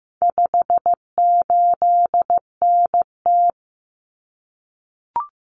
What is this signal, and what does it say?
Morse code, heard as a keyed tone, about 15 wpm, 700 Hz. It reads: 58NT
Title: Common Cut Numbers - 4-Digit 15wpm